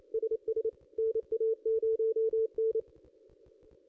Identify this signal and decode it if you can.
Morse code, heard as a keyed tone, about 25 wpm, 440 Hz. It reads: SS NA0N
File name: contests / ss-cw-k5tr-op-n5rz -2200